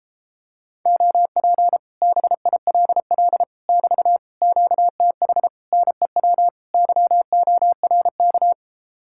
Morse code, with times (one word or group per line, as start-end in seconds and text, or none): 0.9-1.8 OP
2.0-3.4 BILL
3.7-4.2 <BT>
4.4-5.5 QTH
5.7-6.5 NEW
6.7-8.5 YORK